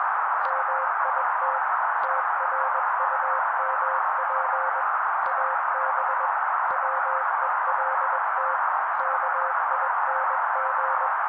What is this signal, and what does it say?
Morse code, heard as a keyed tone, about 21 wpm, 530 Hz. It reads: MIT TRUMP AB WELT KING